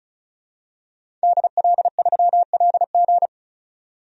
Morse code, heard as a keyed tone, about 36 wpm, 700 Hz. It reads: DL3LG
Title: DXCC-100 Call Signs - 5 Characters - 35wpm